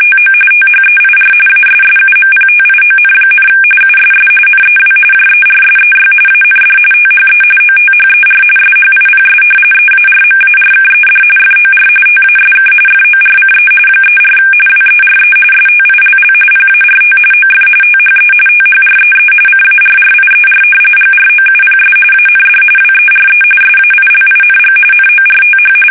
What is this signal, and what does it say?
RTTY, about 45.45 baud. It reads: TO THEIR ASSUMPTION, USS NEW JERSEY AND THE OTHER SHIPS EARMARKED FOR TASK FORCE 34 WERE STILL PART OF THE THREE CARRIER GROUPS AND WERE SAILING NORTH
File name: fsk850 / battleshipnj